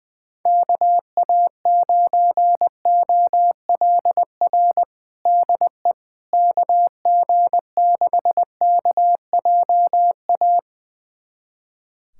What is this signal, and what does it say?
Morse code, heard as a keyed tone, about 20 wpm, 700 Hz. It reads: KA9OLR DE KG6KJA
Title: USA Callsign DE Callsign - 6 Characters - 20wpm